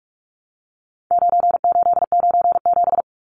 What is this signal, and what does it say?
Morse code, heard as a keyed tone, about 44 wpm, 700 Hz. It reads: 9897
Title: Numbers - 4-Digit 45wpm